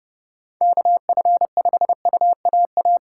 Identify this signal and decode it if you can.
Morse code, heard as a keyed tone, about 30 wpm, 700 Hz. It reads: KF5UAA